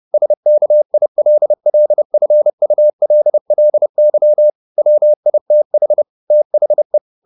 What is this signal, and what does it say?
Morse code, heard as a keyed tone, about 30 wpm, 600 Hz. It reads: SKILLFULLY WITH THE
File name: the old man and the sea / the old man and the sea 1 30wpm